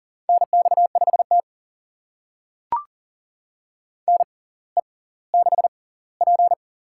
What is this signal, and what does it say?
Morse code, heard as a keyed tone, about 40 wpm, spaced as for 21 wpm, 700 Hz. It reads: NX5T NE6P
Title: USA Call Signs - 4 Characters - 6x Character Spacing - 40wpm